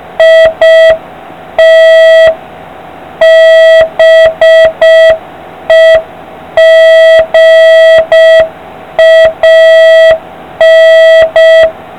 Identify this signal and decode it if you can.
Morse code, heard as a keyed tone, about 6 wpm, 650 Hz, with noise. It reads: IT BEGAN